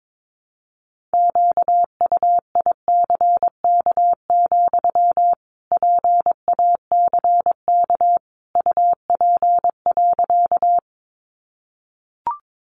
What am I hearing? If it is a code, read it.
Morse code, heard as a keyed tone, about 22 wpm, 700 Hz. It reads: QUICK, PACK UP.
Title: Sentences from Top 1300 Words 22wpm